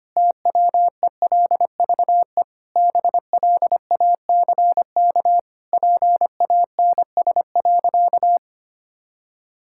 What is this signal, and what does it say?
Morse code, heard as a keyed tone, about 25 wpm, 700 Hz. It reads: TWELVE BLACK PANS.
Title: Pangram Challenge v2 - 25wpm